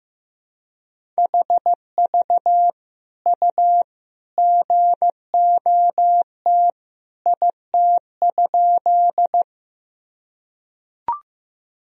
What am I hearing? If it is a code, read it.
Morse code, heard as a keyed tone, about 15 wpm, 700 Hz. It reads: HV U GOT IT?
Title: Sentences from Top 300 Words and Abbreviations 15wpm